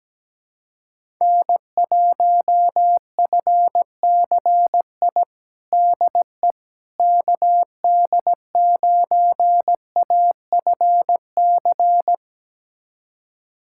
Morse code, heard as a keyed tone, about 17 wpm, 700 Hz. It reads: N1FCI DE KD9AFC
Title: USA Callsign DE Callsign 17wpm